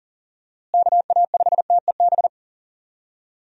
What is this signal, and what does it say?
Morse code, heard as a keyed tone, about 37 wpm, 700 Hz. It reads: KA5TEB
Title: USA Call Signs - 6 Characters - Repeated 2x - 40wpm